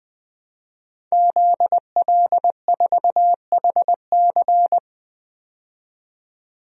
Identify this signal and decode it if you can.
Morse code, heard as a keyed tone, about 20 wpm, 700 Hz. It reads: ZL4HC
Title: DXCC-100 Call Signs - 5 Characters - 20wpm